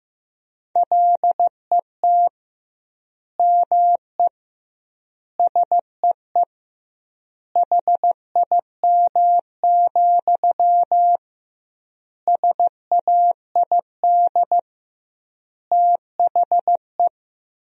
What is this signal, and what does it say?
Morse code, heard as a keyed tone, about 15 wpm, 700 Hz. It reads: LET ME SEE HIM, SAID THE
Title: 2x Word-Spacing - Sentences from Top 400 Words - 15wpm